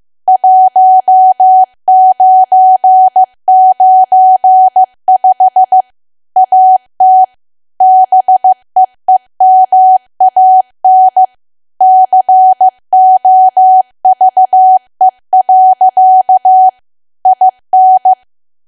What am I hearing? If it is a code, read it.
Morse code, heard as a keyed tone, about 15 wpm, 750 Hz. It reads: 1995 AT BEEMAN COVE. IN
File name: cwa / W1AW-Level2 / 15WPM